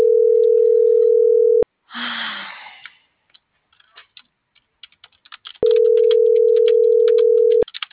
Ringback tone (North America), 440 + 480 Hz.